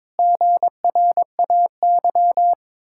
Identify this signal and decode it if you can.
Morse code, heard as a keyed tone, about 22 wpm, 700 Hz. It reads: GRAY
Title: Pangram Challenge v2 - 22wpm